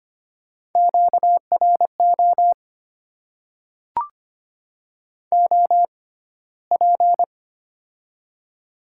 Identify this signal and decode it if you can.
Morse code, heard as a keyed tone, about 25 wpm, spaced as for 15 wpm, 690 Hz. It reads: QRO OP
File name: Top 90 QSO Elements - 6x Character Spacing - 25wpm